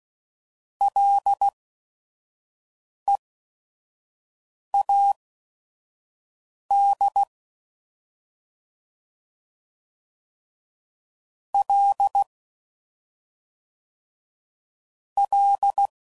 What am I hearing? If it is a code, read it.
Morse code, heard as a keyed tone, about 16 wpm, spaced as for 5 wpm, 800 Hz. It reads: LEAD L L
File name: Prefixes / FLs & THs / Single L's